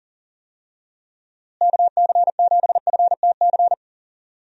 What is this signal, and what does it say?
Morse code, heard as a keyed tone, about 40 wpm, 700 Hz. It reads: KC7FTC
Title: USA Call Signs - 6 Characters - Repeated 3x - 40wpm